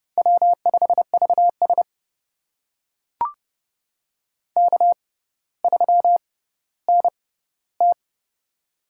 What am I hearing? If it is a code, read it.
Morse code, heard as a keyed tone, about 30 wpm, spaced as for 15 wpm, 700 Hz. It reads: W5VS K3NT
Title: USA Call Signs - 4 Characters - 6x Character Spacing - 30wpm